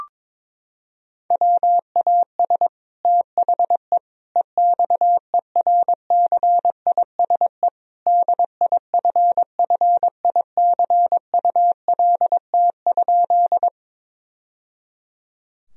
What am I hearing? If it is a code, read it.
Morse code, heard as a keyed tone, about 22 wpm, 700 Hz. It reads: WAS THE EXERCISE DIFFICULT?
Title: Sentences from Top 1000 Words - Review Part 4 - 22wpm